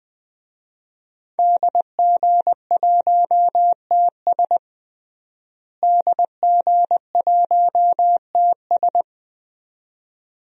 Morse code, heard as a keyed tone, about 20 wpm, 700 Hz. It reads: DG1TS DG1TS
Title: DXCC-100 Call Signs - 5 Characters - Repeated 2x - 20wpm